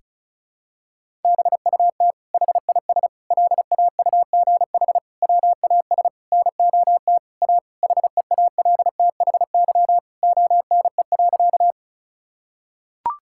Morse code, heard as a keyed tone, about 36 wpm, 700 Hz. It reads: BUT HIS LAUGH WAS NOT A HEALTHY ONE.
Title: Sentences from Top 1000 Words - Review Part 5 - 35wpm